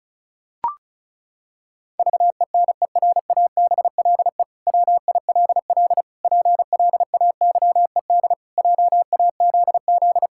Morse code, heard as a keyed tone, about 36 wpm, 700 Hz. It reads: VENERABLE WILL PLAYED JAZZ
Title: Pangram Challenge v2 - 35wpm